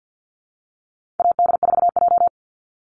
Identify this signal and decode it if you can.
Morse code, heard as a keyed tone, about 50 wpm, 700 Hz. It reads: AD4J
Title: Call Signs Repeated 3x - 50wpm